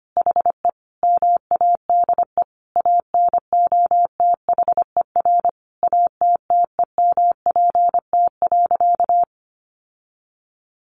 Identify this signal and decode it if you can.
Morse code, heard as a keyed tone, about 25 wpm, 700 Hz. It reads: HE MADE ANOTHER ATTEMPT.
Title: Sentences from Top 1000 Words 25wpm